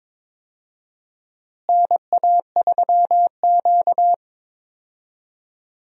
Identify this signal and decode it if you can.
Morse code, heard as a keyed tone, about 21 wpm, 700 Hz. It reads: NA3Q
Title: USA Call Signs - 4 Characters - Repeated 2x - 22wpm